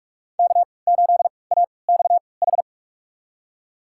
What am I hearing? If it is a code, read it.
Morse code, heard as a keyed tone, about 46 wpm, spaced as for 26 wpm, 700 Hz. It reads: K8AXH
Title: USA Call Signs - 5 Characters - 3x Character Spacing - 45wpm